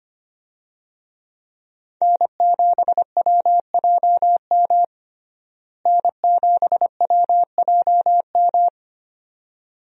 Morse code, heard as a keyed tone, about 25 wpm, 700 Hz. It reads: N7WJM N7WJM
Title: USA Call Signs - 5 Characters - Repeated 2x - 25wpm